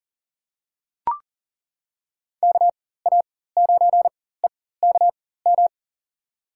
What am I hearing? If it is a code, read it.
Morse code, heard as a keyed tone, about 37 wpm, spaced as for 18 wpm, 700 Hz. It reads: KA9EKM
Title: USA Call Signs - 6 Characters - 4x Character Spacing - 40wpm